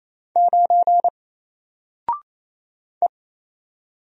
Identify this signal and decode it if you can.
Morse code, heard as a keyed tone, about 27 wpm, 700 Hz. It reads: 9 E
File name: Single-Letter-Number-28wpm